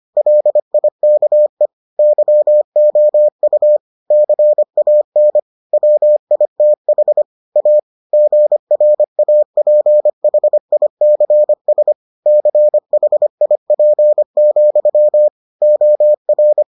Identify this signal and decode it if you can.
Morse code, heard as a keyed tone, about 25 wpm, 600 Hz. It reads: LIKE YOU CAN WITH A GRAPHICS CHIP, OR